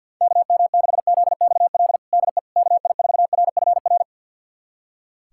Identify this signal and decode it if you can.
Morse code, heard as a keyed tone, about 50 wpm, 700 Hz. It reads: KM6ZXL DE KI4RFR
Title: USA Callsign DE Callsign 50wpm